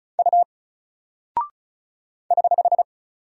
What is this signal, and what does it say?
Morse code, heard as a keyed tone, about 34 wpm, 700 Hz. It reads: U <HH>
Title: Single Character (Letter:Number:Punctuation:Prosign) Alt 35wpm